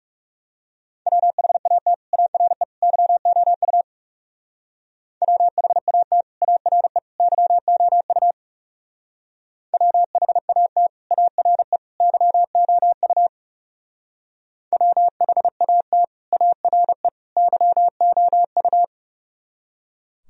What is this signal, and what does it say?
Morse code, heard as a keyed tone, about 37 wpm, 700 Hz. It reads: WHAT ARE YOU WHAT ARE YOU WHAT ARE YOU WHAT ARE YOU
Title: Speed-Racing - Sets of 3 Words 30wpm